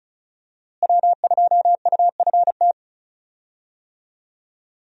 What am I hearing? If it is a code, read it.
Morse code, heard as a keyed tone, about 36 wpm, 700 Hz. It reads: W2UFT
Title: USA Call Signs - 5 Characters - Repeated 3x - 35wpm